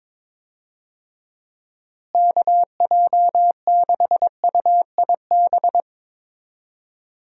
Morse code, heard as a keyed tone, about 22 wpm, 700 Hz. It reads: KJ6UIB
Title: USA Call Signs - 6 Characters - Repeated 3x - 22wpm